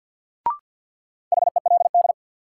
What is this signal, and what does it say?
Morse code, heard as a keyed tone, about 52 wpm, 700 Hz. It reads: HELD
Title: Sentences from Top 1600 Words - New Word Drill Alt - 50wpm